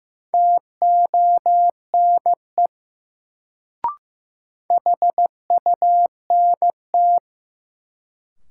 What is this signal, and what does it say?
Morse code, heard as a keyed tone, about 15 wpm, 700 Hz. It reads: TONE HUNT
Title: Sentences from Top 1600 Words - New Word Drill - 15wpm